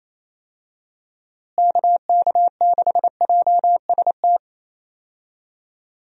Morse code, heard as a keyed tone, about 28 wpm, 700 Hz. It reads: KK6JST
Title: USA Call Signs - 6 Characters - Repeated 2x - 28wpm